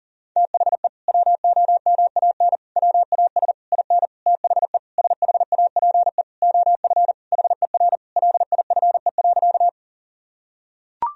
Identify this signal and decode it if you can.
Morse code, heard as a keyed tone, about 37 wpm, 700 Hz. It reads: THE WOMAN WAS IN THE SHAPE OF HER LIFE.
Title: Sentences from Top 1000 Words - Review Part 3 - 40wpm